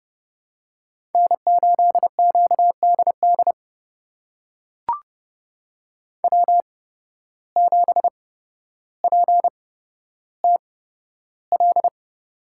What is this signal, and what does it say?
Morse code, heard as a keyed tone, about 30 wpm, spaced as for 13 wpm, 700 Hz. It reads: N8QDD W7PTL
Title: USA Call Signs - 8x Character Spacing - 30wpm